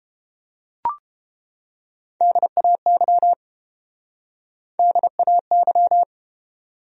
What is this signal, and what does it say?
Morse code, heard as a keyed tone, about 31 wpm, 700 Hz. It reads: DAY DAY